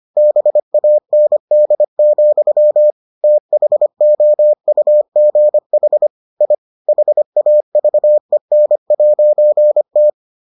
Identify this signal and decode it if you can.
Morse code, heard as a keyed tone, about 25 wpm, 600 Hz. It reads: BAND, THOUGH I HAVEN'T